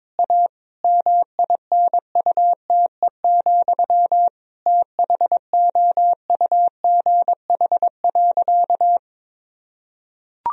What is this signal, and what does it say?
Morse code, heard as a keyed tone, about 22 wpm, 700 Hz. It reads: A MINUTE, THOUGH.
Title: Sentences from Top 1000 Words - Review Part 2 - 22wpm